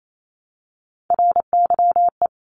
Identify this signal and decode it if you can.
Morse code, heard as a keyed tone, about 28 wpm, 700 Hz. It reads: RYE
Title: Multi-letter Phonemes Repeated 3x 28wpm